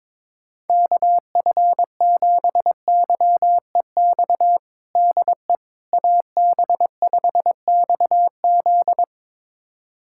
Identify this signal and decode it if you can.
Morse code, heard as a keyed tone, about 22 wpm, 700 Hz. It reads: KF7YEX DE AB5XZ
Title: USA Callsign DE Callsign 22wpm